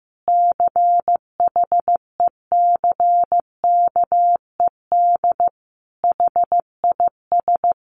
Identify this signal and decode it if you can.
Morse code, heard as a keyed tone, about 15 wpm, 700 Hz. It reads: CHECKED HIS
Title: A Sound Of Thunder 15wpm